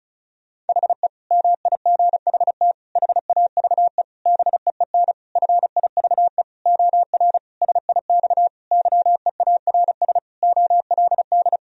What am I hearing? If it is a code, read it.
Morse code, heard as a keyed tone, about 36 wpm, 700 Hz. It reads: HE MIGHT HAVE BEEN FIVE OR SIX YEARS OLD
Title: Sentences from Top 1000 Words - Review Part 2 - 35wpm